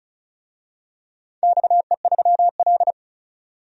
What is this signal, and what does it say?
Morse code, heard as a keyed tone, about 36 wpm, 700 Hz. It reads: XE3L
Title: DXCC-100 Call Signs - 4 Characters - 35wpm